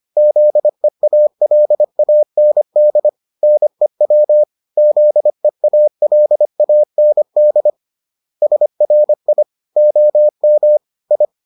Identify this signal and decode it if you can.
Morse code, heard as a keyed tone, about 25 wpm, 600 Hz. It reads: ZEALAND NEW ZEALAND SRI OM I